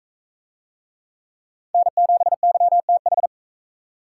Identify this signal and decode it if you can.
Morse code, heard as a keyed tone, about 41 wpm, 700 Hz. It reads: N7YTH